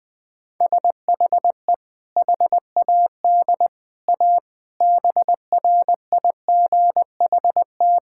Morse code, heard as a keyed tone, about 20 wpm, 700 Hz. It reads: SHE HAD A BRIGHT
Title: Sentences with 2-Word Phrases - 20wpm